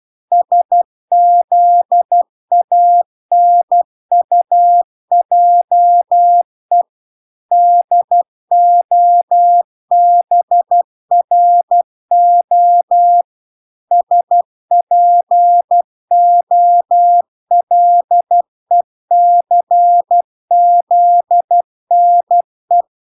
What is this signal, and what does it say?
Morse code, heard as a keyed tone, about 12 wpm, 700 Hz. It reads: SZANUJE DOBRO SPOLECZNE